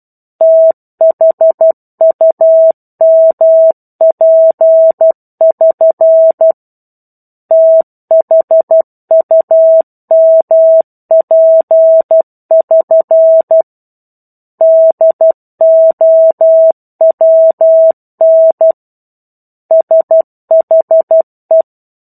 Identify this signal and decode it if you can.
Morse code, heard as a keyed tone, about 12 wpm, 650 Hz. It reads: THUMP<SN> THUMP<SN> DOWN SHE
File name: 01-Chapter-01 12 WPM